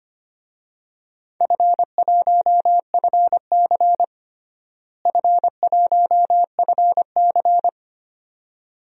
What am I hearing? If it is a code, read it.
Morse code, heard as a keyed tone, about 25 wpm, 700 Hz. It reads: F1FC F1FC
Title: DXCC-100 Call Signs - 4 Characters - Repeated 2x - 25wpm